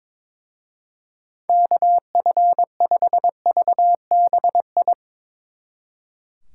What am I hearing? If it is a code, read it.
Morse code, heard as a keyed tone, about 24 wpm, 700 Hz. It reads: KF5VBI
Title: Speed Racing - USA 2x3 Call Signs 22wpm